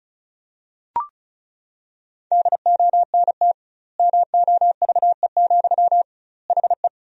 Morse code, heard as a keyed tone, about 36 wpm, 700 Hz. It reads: DONT MOVE, HE
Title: 2x Word-Spacing - Sentences from Top 300 Words - 35wpm